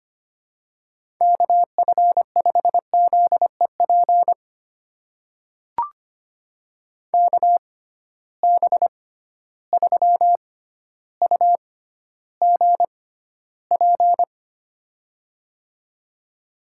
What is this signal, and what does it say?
Morse code, heard as a keyed tone, about 25 wpm, spaced as for 13 wpm, 700 Hz. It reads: KF5ZEP KB3UGP
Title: USA Call Signs - 6 Characters - 6x Character Spacing - 25wpm